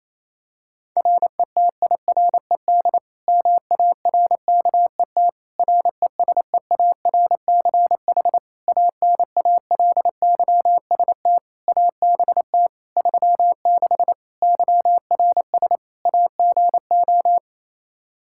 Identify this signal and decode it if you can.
Morse code, heard as a keyed tone, about 27 wpm, 700 Hz. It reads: RETIRED MARKET RESEARCH ANALYST ABT 36 YRS AGO
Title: Practice QSO - Exchange 3 - More Info - 28wpm